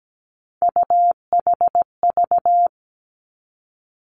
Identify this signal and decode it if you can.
Morse code, heard as a keyed tone, about 17 wpm, 700 Hz. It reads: UHV
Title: Three Challenging Letter-Number Combinations 17wpm